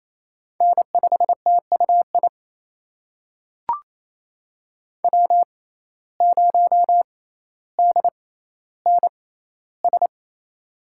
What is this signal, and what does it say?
Morse code, heard as a keyed tone, about 28 wpm, spaced as for 14 wpm, 700 Hz. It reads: N5TUI W0DNS
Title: USA Call Signs - 5 Characters - 6x Character Spacing - 28wpm